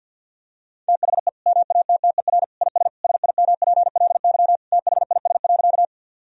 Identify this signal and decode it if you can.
Morse code, heard as a keyed tone, about 50 wpm, 700 Hz. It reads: THE MATTER IS SIMPLY THIS.
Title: Sentences from Top 1000 Words - Review Part 4 - 50wpm